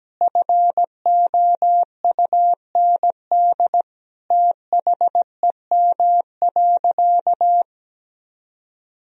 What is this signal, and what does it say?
Morse code, heard as a keyed tone, about 17 wpm, 700 Hz. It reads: FOUND THEM.
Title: Sentences from Top 1000 Words - Review Part 1 - 17wpm